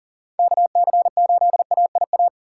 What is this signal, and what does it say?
Morse code, heard as a keyed tone, about 40 wpm, 700 Hz. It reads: KC8AIA
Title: USA Call Signs - 4x Character Spacing - 40wpm